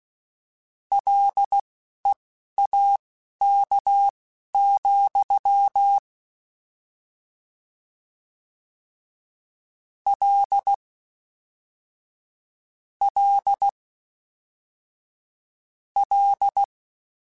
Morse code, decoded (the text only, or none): LEAK, L L L